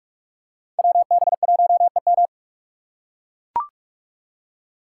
Morse code, heard as a keyed tone, about 44 wpm, 700 Hz. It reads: WB1EM
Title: Code Groups - 5 Letter:Numbers - 45wpm